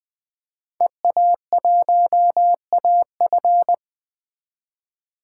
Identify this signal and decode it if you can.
Morse code, heard as a keyed tone, about 21 wpm, 700 Hz. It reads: EA1AF
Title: 2023 POTA Hunter Call Signs - 20wpm